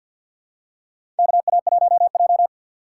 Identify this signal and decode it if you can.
Morse code, heard as a keyed tone, about 52 wpm, 700 Hz. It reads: KA1J
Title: USA Call Signs - 4 Characters - 2x Character Spacing - 50wpm